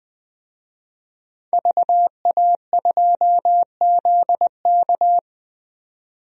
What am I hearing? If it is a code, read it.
Morse code, heard as a keyed tone, about 20 wpm, 700 Hz. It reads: VA2ZK
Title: DXCC-100 Call Signs - 5 Characters - 20wpm